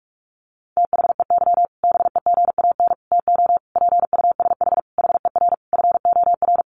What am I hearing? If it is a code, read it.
Morse code, heard as a keyed tone, about 44 wpm, 700 Hz. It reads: THEY BEGAN TO PUSH HER FOR
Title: Sentences from Top 1300 Words 45wpm